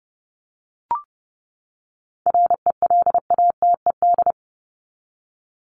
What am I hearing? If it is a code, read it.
Morse code, heard as a keyed tone, about 30 wpm, 700 Hz. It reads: RELATED
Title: ED Suffix - Top 100 Words with Base Words 30wpm